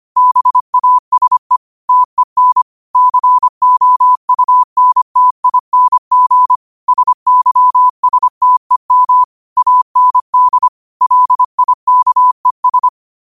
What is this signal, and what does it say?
Morse code, heard as a keyed tone, about 25 wpm, 1000 Hz. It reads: BASE TEN COUNTING SYSTEM AND LIKES
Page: New quotes every day in morse code at 25 Words per minute.